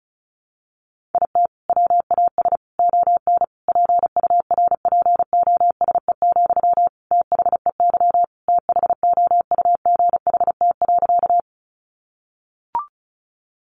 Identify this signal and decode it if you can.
Morse code, heard as a keyed tone, about 34 wpm, 700 Hz. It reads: IT WAS ON PURPOSE, THEY THOUGHT.
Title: Sentences from Top 800 Words 35wpm